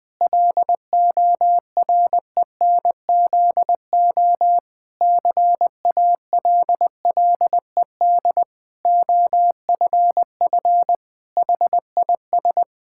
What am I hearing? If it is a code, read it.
Morse code, heard as a keyed tone, about 20 wpm, 700 Hz. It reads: LORENZO CALLED OFF HIS
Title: Pangram Challenge v2 - 20wpm